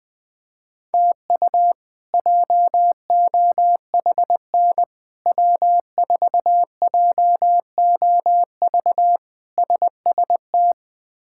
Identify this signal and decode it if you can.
Morse code, heard as a keyed tone, about 21 wpm, 700 Hz. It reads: TU JOHN W4JOV SST
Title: K1USN SST QSO Practice - 20wpm